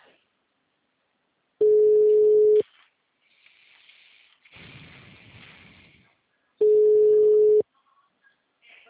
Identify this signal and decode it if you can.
Ringback tone (Europe, 425 Hz).
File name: out- -2024- - - .93